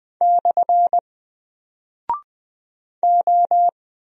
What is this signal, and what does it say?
Morse code, heard as a keyed tone, about 20 wpm, 700 Hz. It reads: / O
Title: Single Character (Letter:Number:Punctuation:Prosign) 20wpm